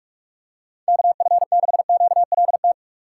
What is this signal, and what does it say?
Morse code, heard as a keyed tone, about 48 wpm, 700 Hz. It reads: KF6QLT